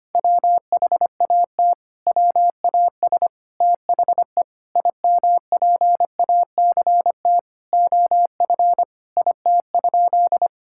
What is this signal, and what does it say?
Morse code, heard as a keyed tone, about 25 wpm, 700 Hz. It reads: WHAT WAS THE IMPACT OF IT?
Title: Sentences from Top 1000 Words - Review Part 6 - 25wpm